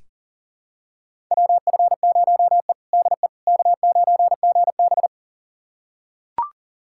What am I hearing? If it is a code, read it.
Morse code, heard as a keyed tone, about 40 wpm, 700 Hz. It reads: WF0E DE K9GB
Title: USA Callsign DE Callsign - 4 Characters - 40wpm